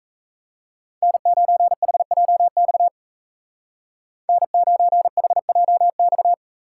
Morse code, heard as a keyed tone, about 34 wpm, 700 Hz. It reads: N9HJX N9HJX
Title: Speed Racing - USA Call Signs 28wpm